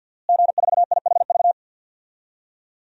Morse code, heard as a keyed tone, about 48 wpm, 700 Hz. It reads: G4IHV
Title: DXCC-100 Call Signs - 5 Characters - Repeated 2x - 50wpm